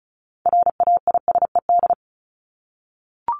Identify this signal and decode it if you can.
Morse code, heard as a keyed tone, about 34 wpm, 700 Hz. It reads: RAISED
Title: ED Suffix - Top 100 Words with Base Words 35wpm